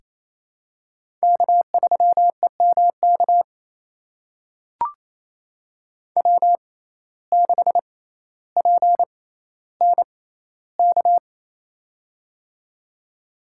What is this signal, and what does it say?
Morse code, heard as a keyed tone, about 27 wpm, spaced as for 14 wpm, 700 Hz. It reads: K3EMK W6PNK
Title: USA Call Signs - 5 Characters - 6x Character Spacing - 28wpm